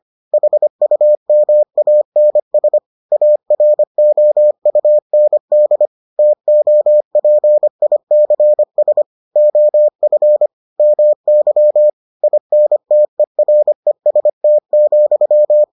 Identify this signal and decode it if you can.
Morse code, heard as a keyed tone, about 25 wpm, 600 Hz. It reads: HUMANS AROUND TOPICS OF MY INTEREST,